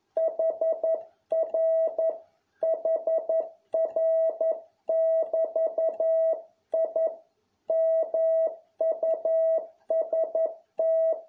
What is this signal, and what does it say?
Morse code, heard as a keyed tone, about 11 wpm, 650 Hz. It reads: HR HR<BT>I MUST